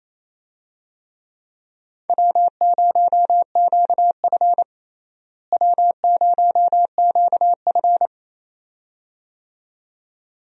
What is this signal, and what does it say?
Morse code, heard as a keyed tone, about 28 wpm, 700 Hz. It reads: W0QF W0QF
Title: USA Call Signs - 4 Characters - Repeated 2x - 28wpm